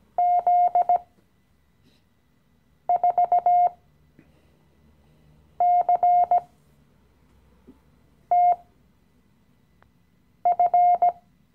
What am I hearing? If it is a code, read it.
Morse code, heard as a keyed tone, about 17 wpm, spaced as for 4 wpm, 710 Hz. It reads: Z4CTF
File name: CW-Online-Kurs / Anfänger / Lektion 015